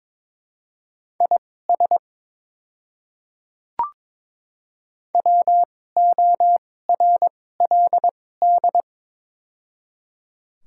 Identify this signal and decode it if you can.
Morse code, heard as a keyed tone, about 22 wpm, spaced as for 16 wpm, 700 Hz. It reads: IS WORLD
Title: 2x Character Spacing - Top 300 Words - 22wpm